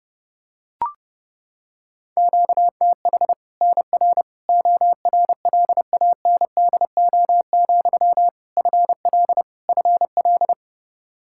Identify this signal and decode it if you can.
Morse code, heard as a keyed tone, about 32 wpm, 700 Hz. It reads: QTH NR ORLANDO, FL FL
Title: Practice QSO - Exchange 1 - Beginning - 30wpm